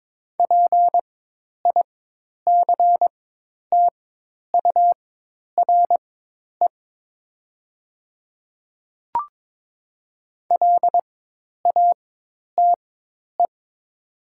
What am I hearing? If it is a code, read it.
Morse code, heard as a keyed tone, about 22 wpm, spaced as for 10 wpm, 700 Hz. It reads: PICTURE LATE
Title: 4x Character Spacing - Top 300 Words - 22wpm